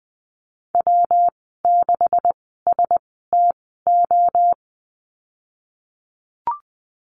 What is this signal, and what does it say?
Morse code, heard as a keyed tone, about 20 wpm, spaced as for 15 wpm, 700 Hz. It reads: W6STO
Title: 154 Lesson 25g - Letter p - Call Signs 20wpm